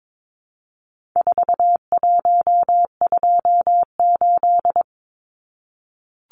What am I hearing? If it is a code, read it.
Morse code, heard as a keyed tone, about 22 wpm, 700 Hz. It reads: 4128
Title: Numbers - 4-Digit 22wpm